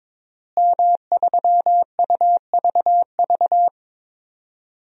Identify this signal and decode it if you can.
Morse code, heard as a keyed tone, about 22 wpm, 700 Hz. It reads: M3UVV
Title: DXCC-100 Call Signs - 5 Characters - Repeated 3x - 22wpm